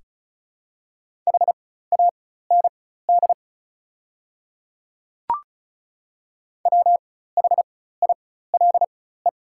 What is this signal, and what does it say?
Morse code, heard as a keyed tone, about 36 wpm, spaced as for 16 wpm, 700 Hz. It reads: HAND WHILE
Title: 4x Character Spacing - Top 300 Words - 35wpm